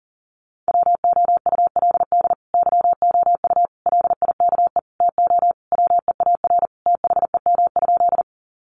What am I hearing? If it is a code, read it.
Morse code, heard as a keyed tone, about 40 wpm, 700 Hz. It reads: WOULD YOU LIKE TO WEAR THEM?
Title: Sentences from Top 1500 Words 40wpm